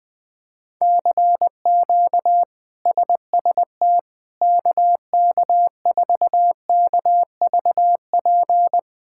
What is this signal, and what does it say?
Morse code, heard as a keyed tone, about 20 wpm, 700 Hz. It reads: CQ SST KK4KVP
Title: K1USN SST QSO Practice - 20wpm